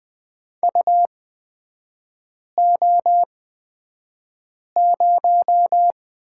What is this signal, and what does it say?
Morse code, heard as a keyed tone, about 20 wpm, spaced as for 5 wpm, 700 Hz. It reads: UO0
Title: 255 Lesson 41e - Period - Three Characters 20wpm